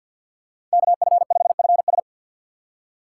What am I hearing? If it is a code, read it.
Morse code, heard as a keyed tone, about 47 wpm, 690 Hz. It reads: KF5FS
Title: USA Call Signs - 5 Characters - 4x Character Spacing - 50wpm